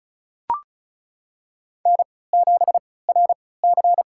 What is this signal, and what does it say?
Morse code, heard as a keyed tone, about 36 wpm, spaced as for 20 wpm, 700 Hz. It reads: N7RC
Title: USA Call Signs - 3x Character Spacing - 35wpm